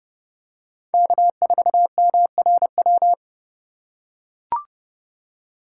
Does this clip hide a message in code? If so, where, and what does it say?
Yes: Morse code from 0.9 to 3.1 s, heard as a keyed tone, about 31 wpm, 700 Hz. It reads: K4MRW